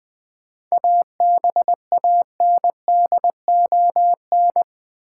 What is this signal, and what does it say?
Morse code, heard as a keyed tone, about 20 wpm, 700 Hz. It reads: ABANDON
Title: 7 Letter Words - First 3 Letters Given - 20wpm